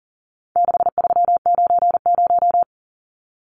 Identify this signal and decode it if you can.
Morse code, heard as a keyed tone, about 40 wpm, 700 Hz. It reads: 6390
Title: Numbers - 4-Digit 40wpm